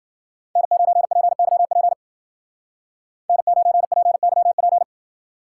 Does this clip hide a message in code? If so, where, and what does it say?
Yes: Morse code from 0.6 to 4.8 s, heard as a keyed tone, about 52 wpm, 700 Hz. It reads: N9PYP N9PYP